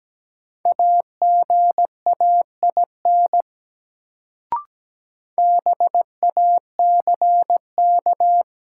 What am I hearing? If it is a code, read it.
Morse code, heard as a keyed tone, about 17 wpm, 700 Hz. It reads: AGAIN BACK
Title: Sentences from Top 200 Words - New Word Drill - 17wpm